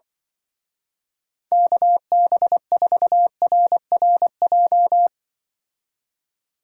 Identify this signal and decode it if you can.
Morse code, heard as a keyed tone, about 24 wpm, 700 Hz. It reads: KB4RRJ
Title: Speed Racing - USA 2x3 Call Signs 20wpm